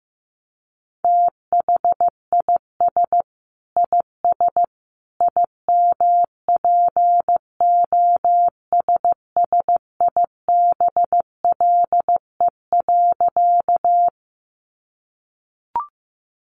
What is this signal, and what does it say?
Morse code, heard as a keyed tone, about 15 wpm, 700 Hz. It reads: THIS IS IMPOSSIBLE.
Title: A Sound Of Thunder 15wpm